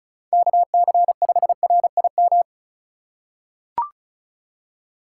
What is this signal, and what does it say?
Morse code, heard as a keyed tone, about 36 wpm, 700 Hz. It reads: KC5RIM
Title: USA Call Signs - 6 Characters - 2x Character Spacing - 35wpm